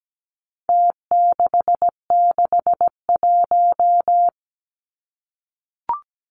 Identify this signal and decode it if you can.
Morse code, heard as a keyed tone, about 17 wpm, 700 Hz. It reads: T661
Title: Numbers - Contest Style - 4-Digit 17wpm